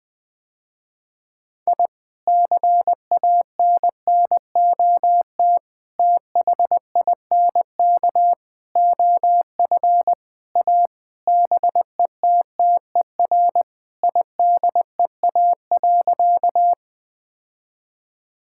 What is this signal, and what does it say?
Morse code, heard as a keyed tone, about 20 wpm, 700 Hz. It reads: I CANNOT THINK OF A BETTER IDEA.
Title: Sentences with 5-Word Phrases - Spoken Beginning - 20wpm